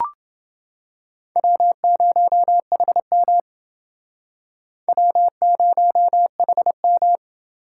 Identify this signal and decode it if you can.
Morse code, heard as a keyed tone, about 29 wpm, 700 Hz. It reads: W0HM W0HM